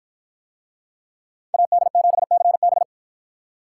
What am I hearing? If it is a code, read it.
Morse code, heard as a keyed tone, about 53 wpm, 700 Hz. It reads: AD7CB